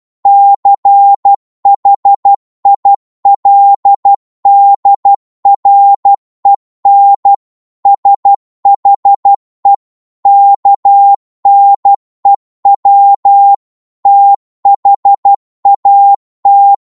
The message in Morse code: CHILDREN SHE KNEW THAT